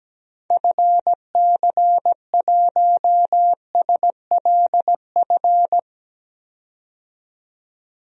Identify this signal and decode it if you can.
Morse code, heard as a keyed tone, about 17 wpm, 700 Hz. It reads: FC1SLF
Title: DXCC-100 Call Signs - 6 Characters - 17wpm